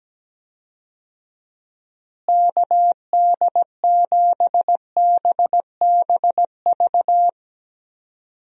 Morse code, heard as a keyed tone, about 17 wpm, 700 Hz. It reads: KD7BBV
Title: USA Call Signs - 6 Characters - Repeated 2x - 17wpm